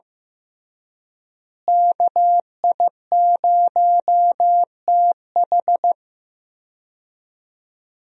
Morse code, heard as a keyed tone, about 15 wpm, 700 Hz. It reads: KI0TH
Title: USA Call Signs - 5 Characters - Repeated 3x - 15wpm